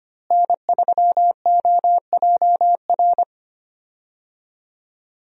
Morse code, heard as a keyed tone, about 25 wpm, 700 Hz. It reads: N3OJR
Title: USA Call Signs - 5 Characters - Repeated 2x - 25wpm